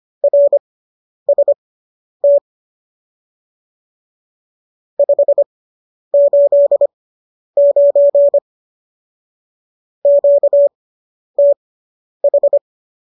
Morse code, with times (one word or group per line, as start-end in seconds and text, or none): 0.2-2.4 RST
5.0-8.4 589
10.1-12.6 QTH